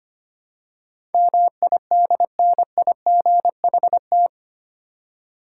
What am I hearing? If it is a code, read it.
Morse code, heard as a keyed tone, about 25 wpm, 700 Hz. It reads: MIDNIGHT
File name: 8 Letter Words - First 3 Letters Given - 25wpm